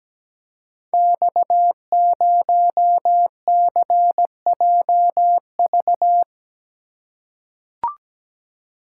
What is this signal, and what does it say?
Morse code, heard as a keyed tone, about 17 wpm, 700 Hz. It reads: X0CJV
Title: Code Groups - 5 Letter:Numbers - 17wpm